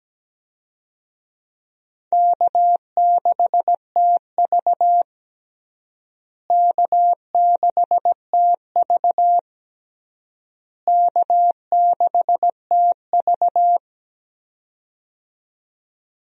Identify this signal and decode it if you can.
Morse code, heard as a keyed tone, about 17 wpm, 700 Hz. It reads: K6TV K6TV K6TV
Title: USA Call Signs - 4 Characters - Repeated 3x - 17wpm